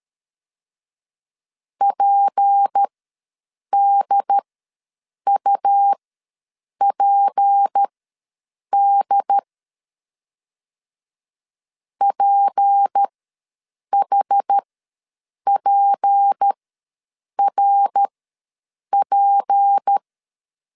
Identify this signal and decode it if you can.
Morse code, heard as a keyed tone, about 13 wpm, spaced as for 7 wpm, 800 Hz. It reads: PDUPD PHPRP
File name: LEZIONI / 6 PKX0 / 6 P ws 12wpm 3117